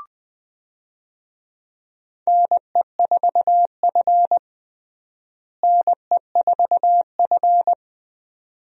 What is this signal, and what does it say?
Morse code, heard as a keyed tone, about 20 wpm, 700 Hz. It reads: NE4F NE4F
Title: USA Call Signs - 4 Characters - Repeated 2x - 20wpm